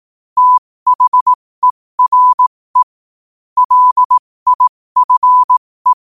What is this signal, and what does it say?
Morse code, heard as a keyed tone, about 18 wpm, spaced as for 15 wpm, 1000 Hz. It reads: THERE LIFE
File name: qotdfiles / 15wpm